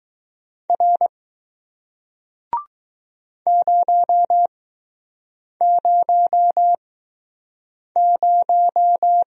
Morse code, heard as a keyed tone, about 20 wpm, spaced as for 6 wpm, 700 Hz. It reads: R 000